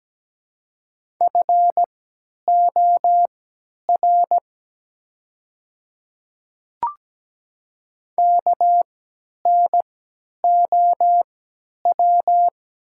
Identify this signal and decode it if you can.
Morse code, heard as a keyed tone, about 17 wpm, spaced as for 10 wpm, 700 Hz. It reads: FOR KNOW
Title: 3x Character Spacing - Top 100 Words - 17wpm